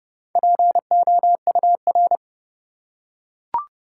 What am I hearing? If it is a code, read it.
Morse code, heard as a keyed tone, about 30 wpm, 700 Hz. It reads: POUR
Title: Four Letter Words - 30wpm